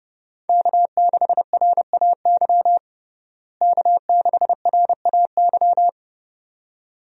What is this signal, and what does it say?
Morse code, heard as a keyed tone, about 30 wpm, 700 Hz. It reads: K6RAY K6RAY
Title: USA Call Signs - 5 Characters - Repeated 2x - 30wpm